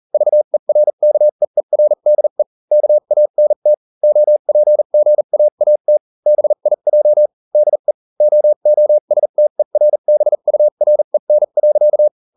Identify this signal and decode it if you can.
Morse code, heard as a keyed tone, about 40 wpm, spaced as for 35 wpm, 600 Hz. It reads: VERKEERDE KANT OPGAAT BIJ DE OOSTERBUREN.